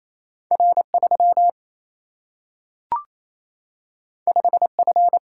Morse code, heard as a keyed tone, about 28 wpm, 700 Hz. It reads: R 3 5 F
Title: Single-Letter Single-Number 28wpm